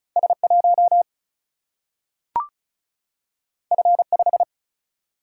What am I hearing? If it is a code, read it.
Morse code, heard as a keyed tone, about 36 wpm, 700 Hz. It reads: S1 F5
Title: Single-Letter Single-Number 35wpm